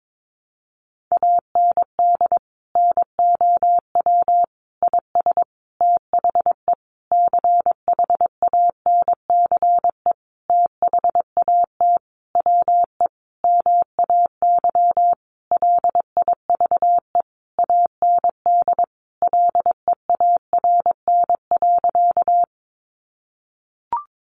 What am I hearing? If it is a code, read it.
Morse code, heard as a keyed tone, about 22 wpm, 700 Hz. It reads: AND NOW IS THE CHANCE THAT WE MAY LIVE AND LEARN.
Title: Sentences from Top 800 Words 22wpm